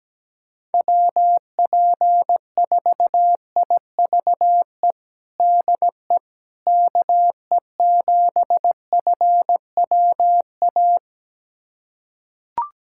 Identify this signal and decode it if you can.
Morse code, heard as a keyed tone, about 17 wpm, 700 Hz. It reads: WP4IVE DE KE7FWA
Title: USA Callsign DE Callsign 17wpm